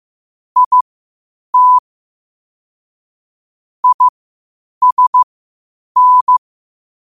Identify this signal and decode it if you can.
Morse code, heard as a keyed tone, about 15 wpm, spaced as for 8 wpm, 1000 Hz. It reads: IT ISN
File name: qotdfiles / 8wpm